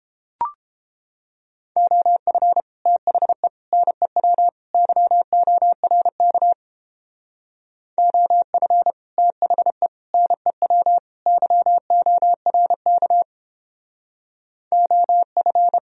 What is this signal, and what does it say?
Morse code, heard as a keyed tone, about 31 wpm, 700 Hz. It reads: OF THE NEW YORK OF THE NEW YORK OF